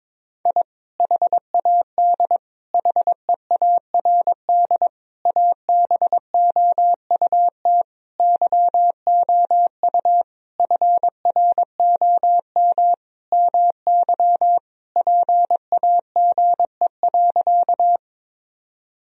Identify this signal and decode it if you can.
Morse code, heard as a keyed tone, about 22 wpm, 700 Hz. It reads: I HAD HEARD ABOUT YOU FROM MY PAGE.
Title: Sentences from Top 1000 Words - Review Part 1 - 22wpm